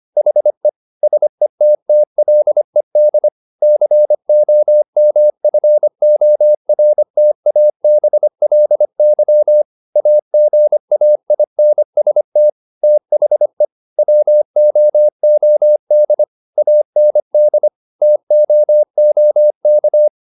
Morse code, heard as a keyed tone, about 25 wpm, 600 Hz. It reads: HE SETTLED COMFORTABLY AGAINST THE WOOD AND TOOK
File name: the old man and the sea part 1 25wpm